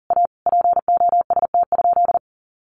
Morse code, heard as a keyed tone, about 40 wpm, 700 Hz. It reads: A POST?
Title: Sentences from Top 1000 Words 40wpm